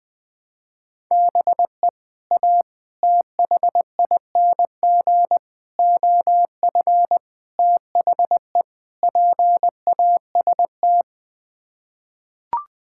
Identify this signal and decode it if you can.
Morse code, heard as a keyed tone, about 20 wpm, 700 Hz. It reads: BE A THING OF THE PAST
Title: Sets of 6 Words - 20wpm